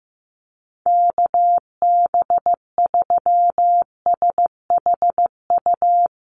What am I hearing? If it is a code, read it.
Morse code, heard as a keyed tone, about 15 wpm, 700 Hz. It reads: KB3SHU
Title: USA Call Signs Repeated 3x 15wpm